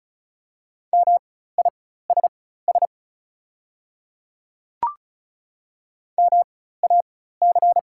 Morse code, heard as a keyed tone, about 34 wpm, spaced as for 16 wpm, 700 Hz. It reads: MISS MAC
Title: 4x Character Spacing - Top 500 Words - 35wpm